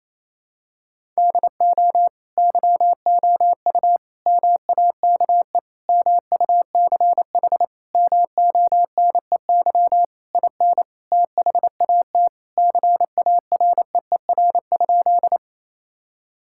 Morse code, heard as a keyed tone, about 27 wpm, 700 Hz. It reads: DO YOU MAKE MUCH MONEY IN THAT CAREER?
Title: Sentences from Top 1000 Words - Review Part 6 - 28wpm